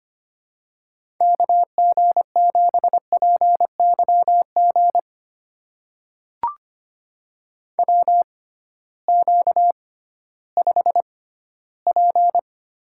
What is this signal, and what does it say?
Morse code, heard as a keyed tone, about 25 wpm, spaced as for 15 wpm, 700 Hz. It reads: KG7PYG WQ5P
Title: USA Call Signs - 6x Character Spacing - 25wpm